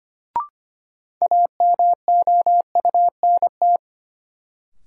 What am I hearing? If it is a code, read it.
Morse code, heard as a keyed tone, about 25 wpm, 700 Hz. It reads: AMOUNT
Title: Sentences from Top 700 Words - New Word Drill - 25wpm